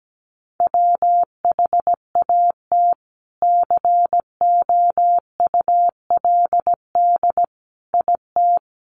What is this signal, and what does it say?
Morse code, heard as a keyed tone, about 17 wpm, 700 Hz. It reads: WHAT COULD IT
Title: Sentences from Top 100 Words 17wpm